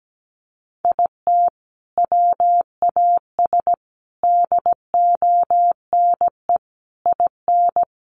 Morse code, heard as a keyed tone, about 17 wpm, 700 Hz. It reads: IT WAS DONE IN
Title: Sentences from Top 1400 Words 17wpm